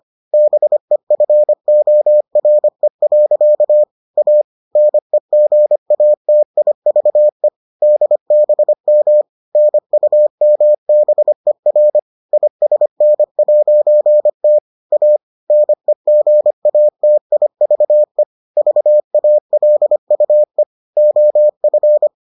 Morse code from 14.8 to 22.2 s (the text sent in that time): A NEGATIVE VALUE OF